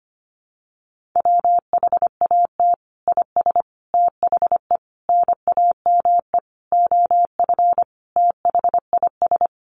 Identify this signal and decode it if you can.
Morse code, heard as a keyed tone, about 25 wpm, 700 Hz. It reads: WHAT IS THE NAME OF THIS
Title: Sentences from Top 1500 Words 25wpm